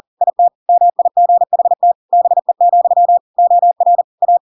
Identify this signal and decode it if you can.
Morse code, heard as a keyed tone, about 40 wpm, 700 Hz. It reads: IT MIGHT BE, OR A
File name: Pinocchio / Pinocchio ch5 40 40 ik1hge